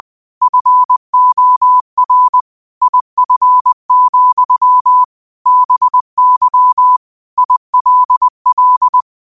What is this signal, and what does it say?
Morse code, heard as a keyed tone, about 20 wpm, 1000 Hz. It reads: FOR IF, BY ILL